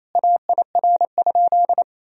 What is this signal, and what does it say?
Morse code, heard as a keyed tone, about 28 wpm, 700 Hz. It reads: AIR?
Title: Sentences from Top 1000 Words - Review Part 2 - 28wpm